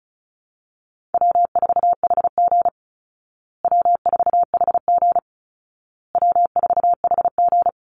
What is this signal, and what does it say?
Morse code, heard as a keyed tone, about 34 wpm, 700 Hz. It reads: W4HG W4HG W4HG
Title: Easy Call Signs Repeated 3x - 35wpm